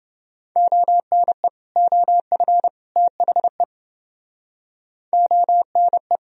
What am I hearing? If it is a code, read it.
Morse code, heard as a keyed tone, about 30 wpm, 700 Hz. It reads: ONE OF THE ONE